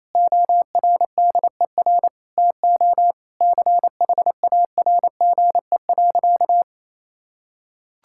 Morse code, heard as a keyed tone, about 28 wpm, 700 Hz. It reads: ORDER TO CHARGE.
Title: Sentences from Top 1000 Words - Review Part 6 - 28wpm